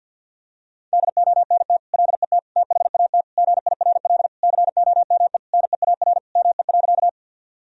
Morse code, heard as a keyed tone, about 51 wpm, 700 Hz. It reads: DONT LET THAT GIRL COME NEAR ME.
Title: Sentences from Top 1000 Words - Review Part 1 - 50wpm